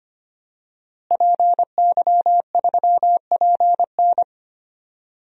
Morse code, heard as a keyed tone, about 25 wpm, 700 Hz. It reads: PY3PN
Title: DXCC-100 Call Signs - 5 Characters - Repeated 2x - 25wpm